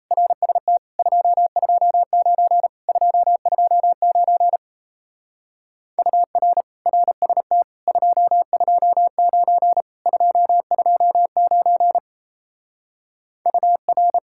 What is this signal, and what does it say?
Morse code, heard as a keyed tone, about 37 wpm, 700 Hz. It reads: RST 229 229 UR RST 229 229 UR